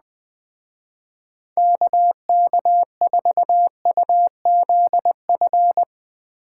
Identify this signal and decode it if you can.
Morse code, heard as a keyed tone, about 20 wpm, 700 Hz. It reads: KK4UZF